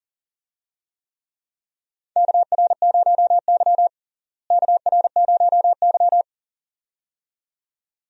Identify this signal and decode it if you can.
Morse code, heard as a keyed tone, about 38 wpm, 700 Hz. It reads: KR0Y KR0Y
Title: USA Call Signs - 4 Characters - Repeated 2x - 40wpm